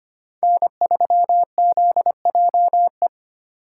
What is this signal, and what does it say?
Morse code, heard as a keyed tone, about 25 wpm, 700 Hz. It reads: N3ZJE
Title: USA Call Signs - 5 Characters - 8x Character Spacing - 25wpm